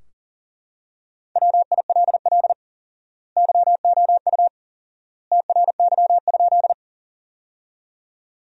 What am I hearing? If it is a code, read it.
Morse code, heard as a keyed tone, about 40 wpm, 700 Hz. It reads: WILL YOU TRY?
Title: 4x Word-Spacing - Sentences from Top 200 Words - 40wpm